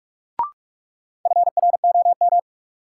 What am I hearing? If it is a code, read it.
Morse code, heard as a keyed tone, about 43 wpm, 700 Hz. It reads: FROM
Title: Sentences from Top 100 Words - New Word Drill - 45wpm